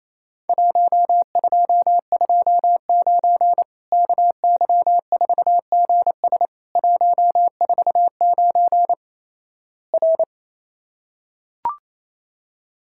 Morse code, heard as a keyed tone, about 27 wpm, 700 Hz. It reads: 1229 KY4GS 149 R
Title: Worked All Europe DX Contest - QTC Exchange - 28wpm